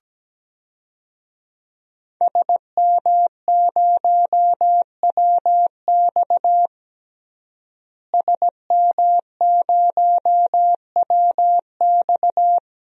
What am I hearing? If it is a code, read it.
Morse code, heard as a keyed tone, about 17 wpm, 700 Hz. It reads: SM0WX SM0WX
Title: DXCC-100 Call Signs - 5 Characters - Repeated 2x - 17wpm